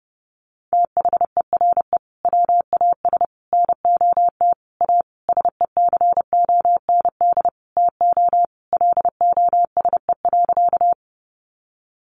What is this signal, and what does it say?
Morse code, heard as a keyed tone, about 30 wpm, 700 Hz. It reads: THERE WAS NOT A SECOND TO LOSE.
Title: Sentences from Top 1100 Words 30wpm